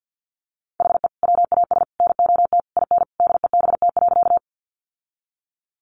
Morse code, heard as a keyed tone, about 50 wpm, 700 Hz. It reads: HE WAS NOT IN DEBT.
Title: Sentences from Top 1300 Words 50wpm